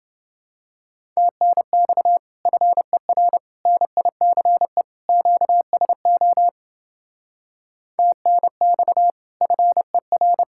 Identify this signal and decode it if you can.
Morse code, heard as a keyed tone, about 28 wpm, 700 Hz. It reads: TNX FER NICE QSO TNX FER